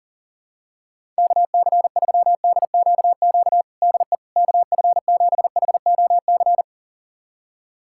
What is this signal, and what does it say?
Morse code, heard as a keyed tone, about 37 wpm, 700 Hz. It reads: KC3DQQ DE KF7HOC
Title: USA Callsign DE Callsign 40wpm